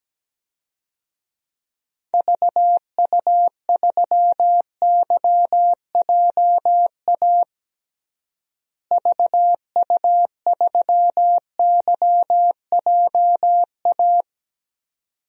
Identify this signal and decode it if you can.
Morse code, heard as a keyed tone, about 17 wpm, 700 Hz. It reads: VU3YJA VU3YJA
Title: DXCC-100 Call Signs - 6 Characters - Repeated 2x - 17wpm